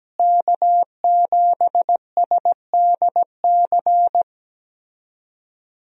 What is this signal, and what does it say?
Morse code, heard as a keyed tone, about 17 wpm, 700 Hz. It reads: K7SDC
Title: USA Call Signs - 5 Characters - Repeated 3x - 17wpm